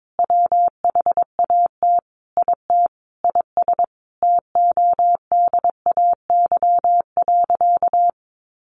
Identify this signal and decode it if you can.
Morse code, heard as a keyed tone, about 22 wpm, 700 Hz. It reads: WHAT IT IS TODAY.
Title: Sentences from Top 1500 Words 22wpm